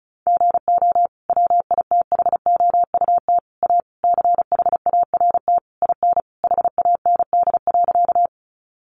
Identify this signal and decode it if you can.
Morse code, heard as a keyed tone, about 34 wpm, 700 Hz. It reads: GO WITHOUT A CHART IN HAND.
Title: Sentences from Top 1300 Words 35wpm